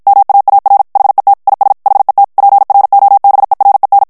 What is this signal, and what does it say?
Morse code, heard as a keyed tone, about 54 wpm, 800 Hz. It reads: MAAR HET IS HET PROBEREN
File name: HOLLANDAIS texte clair 2 vitesse 50 mots minute